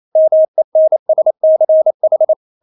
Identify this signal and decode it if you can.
Morse code, heard as a keyed tone, about 28 wpm, 630 Hz. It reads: MENSCH
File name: Witz9 28wpm